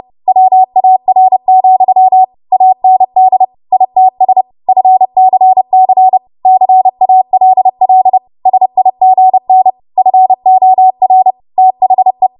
Morse code, heard as a keyed tone, about 30 wpm, 750 Hz. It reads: WAR, AND ITS FCC CALL SIGN FOR THE
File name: Archive / 30 WPM / 30WPM